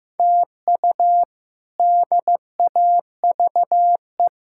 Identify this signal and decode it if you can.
Morse code, heard as a keyed tone, about 15 wpm, 700 Hz. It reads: TU DAVE
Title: K1USN SST QSO Practice - 15wpm